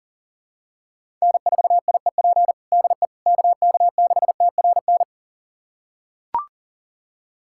Morse code, heard as a keyed tone, about 37 wpm, 700 Hz. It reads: N4IEP DE KK6TRN
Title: USA Callsign DE Callsign 40wpm